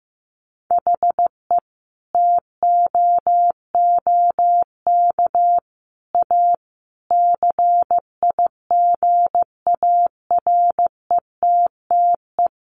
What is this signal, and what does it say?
Morse code, heard as a keyed tone, about 15 wpm, 700 Hz. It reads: HE TOOK A CIGARETTE
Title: Sentences from Top 1000 Words 15wpm